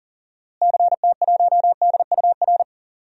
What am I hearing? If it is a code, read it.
Morse code, heard as a keyed tone, about 40 wpm, 700 Hz. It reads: CT1DUR
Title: DXCC-100 Call Signs - 6 Characters - Repeated 3x - 40wpm